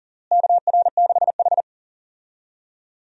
Morse code, heard as a keyed tone, about 40 wpm, 700 Hz. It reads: KR6H
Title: USA Call Signs - 4 Characters - Repeated 3x - 40wpm